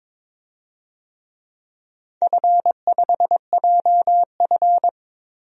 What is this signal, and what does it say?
Morse code, heard as a keyed tone, about 22 wpm, 700 Hz. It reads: F5JF
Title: DXCC-100 Call Signs - 4 Characters - 22wpm